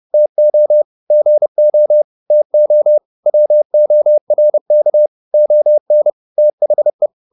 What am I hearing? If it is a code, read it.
Morse code, heard as a keyed tone, about 30 wpm, 600 Hz. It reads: TO GO TO WORK ON THE
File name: the old man and the sea 2 30wpm